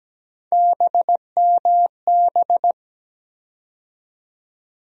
Morse code, heard as a keyed tone, about 17 wpm, 700 Hz. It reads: BMB
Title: USA Call Sign Postfixes - 3 Letters 17wpm